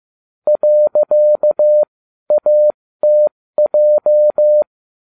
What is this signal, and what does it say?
Morse code, heard as a keyed tone, about 15 wpm, spaced as for 12 wpm, 600 Hz. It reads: .ATJ